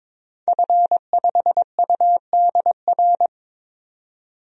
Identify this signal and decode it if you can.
Morse code, heard as a keyed tone, about 22 wpm, 700 Hz. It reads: F5UDR
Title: DXCC-100 Call Signs - 5 Characters - 22wpm